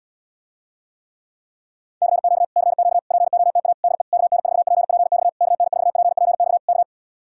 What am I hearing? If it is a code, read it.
Morse code, heard as a keyed tone, about 77 wpm, 700 Hz. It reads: CQ CQ POTA DE KN4PPX KN4PPX K